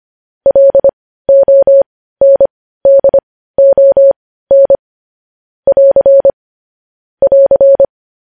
Morse code, heard as a keyed tone, about 25 wpm, spaced as for 15 wpm, 550 Hz. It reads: LONDON <AR> <AR>
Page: news-headlines-in-morse-code-at-15-wpm